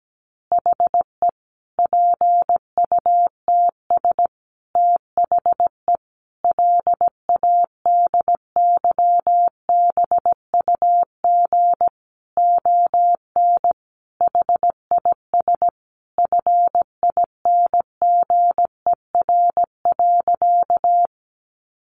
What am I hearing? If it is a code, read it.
Morse code, heard as a keyed tone, about 17 wpm, 700 Hz. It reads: HE PUTS THE LADYBUG ON HIS FINGER.
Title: Additional Easy Phrases 17wpm